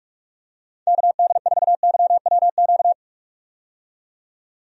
Morse code, heard as a keyed tone, about 46 wpm, 700 Hz. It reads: KD4YWQ
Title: USA Call Signs - 6 Characters - 45wpm